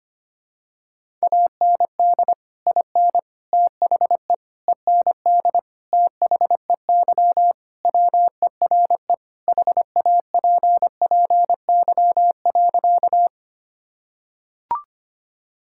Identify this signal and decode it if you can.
Morse code, heard as a keyed tone, about 25 wpm, 700 Hz. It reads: AND IN THE END THEY WERE HAPPY.
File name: practice / 25wpm / Sentences with 2-Word Phrases - 25wpm